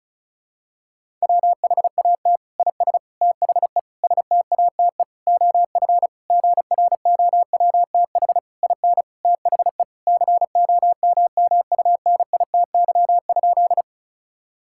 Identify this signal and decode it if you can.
Morse code, heard as a keyed tone, about 34 wpm, 700 Hz. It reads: WHAT IS THE STATE OF GROWTH IN THE COMMUNITY?
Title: Sentences from Top 1000 Words - Review Part 5 - 35wpm